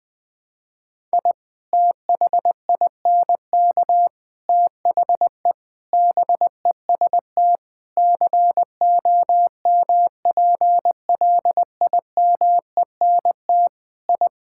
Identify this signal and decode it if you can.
Morse code, heard as a keyed tone, about 20 wpm, 700 Hz. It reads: I THINK THE BEST COMPLIMENT I
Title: Sets of 8 Words - 20wpm